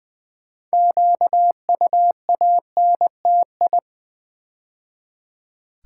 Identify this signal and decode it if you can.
Morse code, heard as a keyed tone, about 20 wpm, 700 Hz. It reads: QUANTI
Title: Build A Word - 3 Letters At A Time - 8-Letter Words - 20wpm